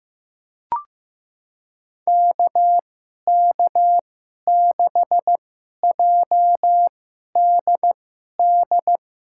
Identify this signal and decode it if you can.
Morse code, heard as a keyed tone, about 15 wpm, spaced as for 11 wpm, 700 Hz. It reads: KK6JDD
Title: USA Call Signs - 6 Characters - 2x Character Spacing - 15wpm